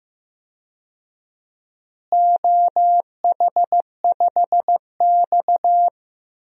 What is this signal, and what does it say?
Morse code, heard as a keyed tone, about 15 wpm, 700 Hz. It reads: OH5X
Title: DXCC-100 Call Signs - 4 Characters - Repeated 2x - 15wpm